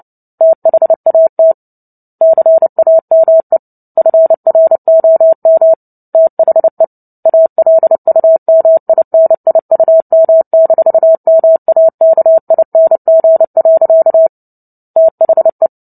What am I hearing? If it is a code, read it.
Morse code, heard as a keyed tone, about 30 wpm, 650 Hz. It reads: THAT CAME FROM THE ALUMINIUM-MAKING. THE
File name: Book Two / 03-Chapter 3 - 29 WPM